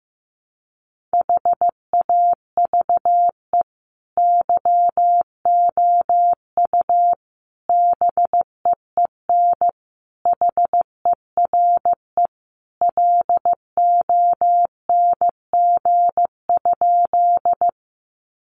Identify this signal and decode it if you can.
Morse code, heard as a keyed tone, about 15 wpm, 700 Hz. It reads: HAVE YOU BEEN HERE LONG?
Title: Sentences from Top 200 Words 15wpm